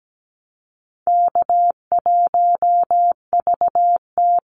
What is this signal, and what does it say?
Morse code, heard as a keyed tone, about 17 wpm, 700 Hz. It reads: K1VT
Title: Easy Call Signs Repeated 3x - 17wpm